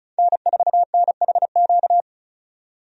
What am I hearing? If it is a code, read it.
Morse code, heard as a keyed tone, about 34 wpm, 700 Hz. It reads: N4NHQ
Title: USA Call Signs - 5 Characters - 35wpm